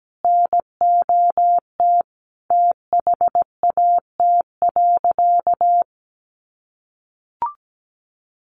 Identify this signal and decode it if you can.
Morse code, heard as a keyed tone, about 17 wpm, 700 Hz. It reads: NOT THAT.
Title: Sentences from Top 600 Words 17wpm